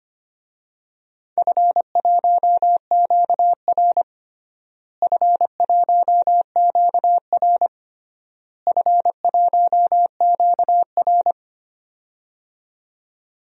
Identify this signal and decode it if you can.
Morse code, heard as a keyed tone, about 25 wpm, 700 Hz. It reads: F1QR F1QR F1QR
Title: DXCC-100 Call Signs - 4 Characters - Repeated 3x - 25wpm